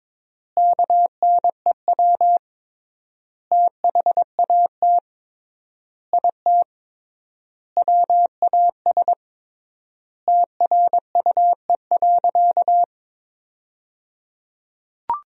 Morse code, heard as a keyed tone, about 22 wpm, 700 Hz. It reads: KNEW THAT IT WAS TRUE.
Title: 3x Word-Spacing - Sentences from Top 400 Words - 22wpm